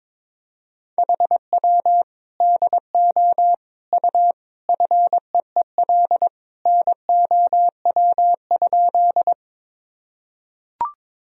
Morse code, heard as a keyed tone, about 22 wpm, 700 Hz. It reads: HW DO U FEEL NOW?
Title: Sentences from Top 400 Words and Abbreviations 22wpm